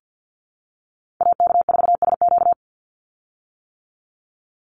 Morse code, heard as a keyed tone, about 48 wpm, 700 Hz. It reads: AK4SQ
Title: Call Signs - 50wpm